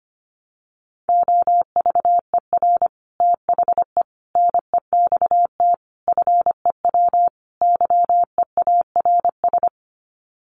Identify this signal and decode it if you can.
Morse code, heard as a keyed tone, about 25 wpm, 700 Hz. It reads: OVER THE NEXT FEW YEARS
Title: Sets of 5 Words Encore 25wpm